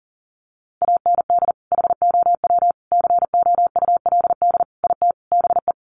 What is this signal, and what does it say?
Morse code, heard as a keyed tone, about 40 wpm, 700 Hz. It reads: AND HOW COULD IT BE
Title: Sentences from Top 1200 Words 40wpm